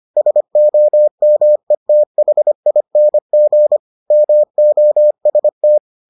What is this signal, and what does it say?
Morse code, heard as a keyed tone, about 25 wpm, 600 Hz. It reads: SOMETHING MOST